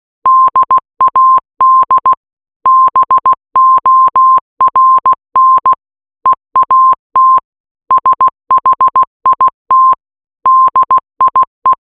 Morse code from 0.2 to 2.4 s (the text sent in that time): DAD